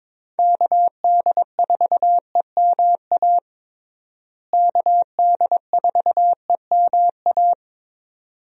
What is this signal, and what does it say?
Morse code, heard as a keyed tone, about 22 wpm, 700 Hz. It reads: KD4EMA KD4EMA
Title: USA Call Signs - 6 Characters - Repeated 2x - 22wpm